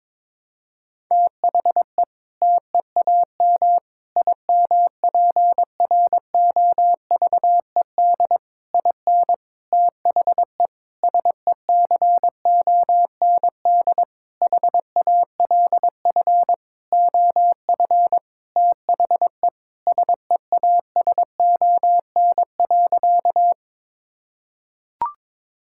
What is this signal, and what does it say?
Morse code, heard as a keyed tone, about 22 wpm, 700 Hz. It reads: THE TEAM IMPROVED IN THE SECOND HALF OF THE SEASON.
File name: Sentences with 7-Word Phrases - 22wpm